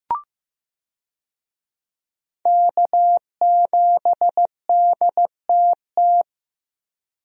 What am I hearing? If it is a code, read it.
Morse code, heard as a keyed tone, about 15 wpm, 700 Hz. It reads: K7DTT
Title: USA Call Signs - 5 Characters - Repeated 3x - 15wpm